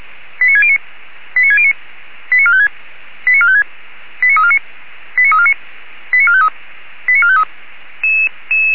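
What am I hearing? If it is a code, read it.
Five-tone selective call (ZVEI), heard as quick runs of five tones, tones of about 70 ms. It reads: 89609 89609 89456 89456 89349 89349 89453 89453
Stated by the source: Five-tone ZVEI selective call received on an SDR.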